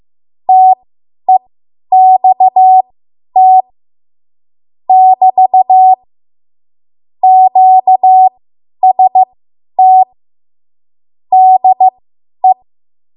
Morse code, heard as a keyed tone, about 15 wpm, spaced as for 10 wpm, 750 Hz. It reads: TEXT <BT> QST DE
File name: Morse / 10WPM